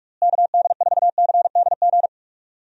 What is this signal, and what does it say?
Morse code, heard as a keyed tone, about 46 wpm, 700 Hz. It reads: KD4CDG
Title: USA Call Signs - 6 Characters - Repeated 2x - 45wpm